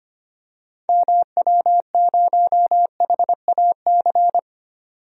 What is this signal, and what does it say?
Morse code, heard as a keyed tone, about 25 wpm, 700 Hz. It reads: MW0HAC
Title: DXCC-100 Call Signs - 6 Characters - 25wpm